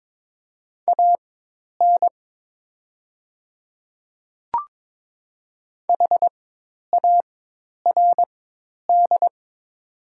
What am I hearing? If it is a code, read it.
Morse code, heard as a keyed tone, about 22 wpm, spaced as for 10 wpm, 700 Hz. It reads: AN HARD
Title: 4x Character Spacing - Top 300 Words - 22wpm